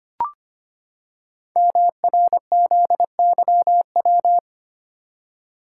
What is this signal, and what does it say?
Morse code, heard as a keyed tone, about 25 wpm, 700 Hz. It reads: MRZYW
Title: Code Groups - 5 Letters - 25wpm